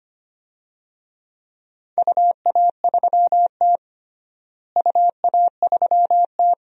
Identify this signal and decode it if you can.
Morse code, heard as a keyed tone, about 25 wpm, 700 Hz. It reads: UA3T UA3T
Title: DXCC-100 Call Signs - 4 Characters - Repeated 2x - 25wpm